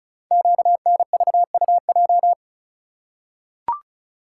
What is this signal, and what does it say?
Morse code, heard as a keyed tone, about 36 wpm, 700 Hz. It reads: QNVUJ
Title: Code Groups - 5 Letters - 35wpm